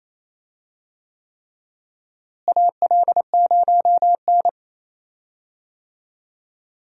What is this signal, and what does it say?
Morse code, heard as a keyed tone, about 28 wpm, 700 Hz. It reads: AL0N
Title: USA Call Signs - 4 Characters - Repeated 3x - 28wpm